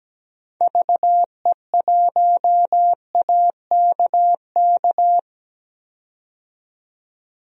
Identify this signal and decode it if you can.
Morse code, heard as a keyed tone, about 17 wpm, 700 Hz. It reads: VE1AKK
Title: DXCC-100 Call Signs - 6 Characters - 17wpm